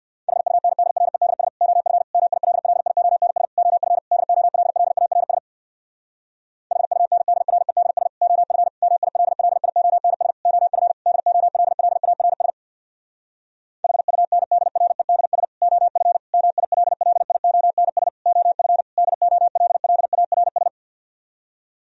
Morse code, heard as a keyed tone, about 65 wpm, 700 Hz. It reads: HUNDREDS OF MILLIONS OF DOLLARS HUNDREDS OF MILLIONS OF DOLLARS HUNDREDS OF MILLIONS OF DOLLARS